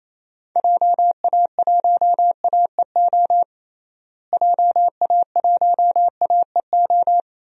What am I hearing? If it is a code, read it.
Morse code, heard as a keyed tone, about 28 wpm, 700 Hz. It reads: JA1AEO JA1AEO
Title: DXCC-100 Call Signs - 6 Characters - Repeated 2x - 28wpm